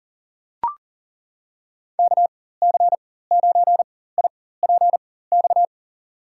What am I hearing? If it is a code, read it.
Morse code, heard as a keyed tone, about 40 wpm, spaced as for 19 wpm, 700 Hz. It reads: KC9IPX
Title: USA Call Signs - 6 Characters - 4x Character Spacing - 40wpm